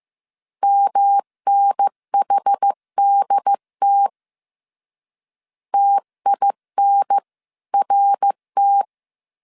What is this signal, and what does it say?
Morse code, heard as a keyed tone, about 15 wpm, 800 Hz. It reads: MNHDT TINRT
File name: CW / DET / 1 3 1500C S C 14wpm 2458